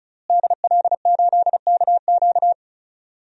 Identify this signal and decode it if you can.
Morse code, heard as a keyed tone, about 34 wpm, 700 Hz. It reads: DL8KQ
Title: DXCC-100 Call Signs - 5 Characters - Repeated 3x - 35wpm